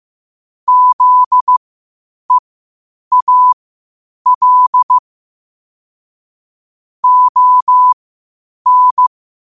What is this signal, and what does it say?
Morse code, heard as a keyed tone, about 15 wpm, spaced as for 8 wpm, 1000 Hz. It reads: ZEAL ON
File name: qotdfiles / 8wpm